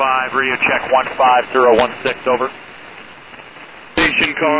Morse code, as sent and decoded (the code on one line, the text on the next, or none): none